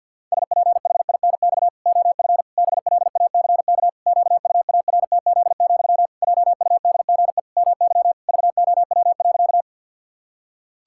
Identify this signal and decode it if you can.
Morse code, heard as a keyed tone, about 50 wpm, 700 Hz. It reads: SPHINX OF BLACK QUARTZ, JUDGE MY VOW.
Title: Pangram Challenge v2 - 50wpm